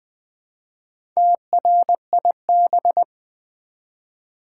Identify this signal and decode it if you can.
Morse code, heard as a keyed tone, about 20 wpm, 700 Hz. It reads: TRIB
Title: Build A Word - 1 Letter At A Time - 5-Letter Words - 20wpm